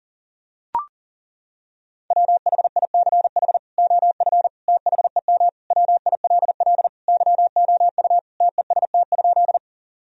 Morse code, heard as a keyed tone, about 40 wpm, 700 Hz. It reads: WHICH OF THEM WILL YOU TEST?
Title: Sentences from Top 1000 Words - Review Part 3 - 40wpm